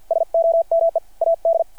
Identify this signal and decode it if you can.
Morse code, heard as a keyed tone, about 50 wpm, 660 Hz. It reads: SOME AD
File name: archived / CRSnet